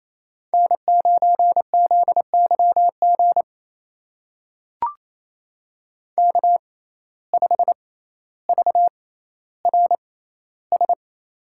Morse code, heard as a keyed tone, about 28 wpm, spaced as for 14 wpm, 700 Hz. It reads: N9ZYG K5VRS
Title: USA Call Signs - 5 Characters - 6x Character Spacing - 28wpm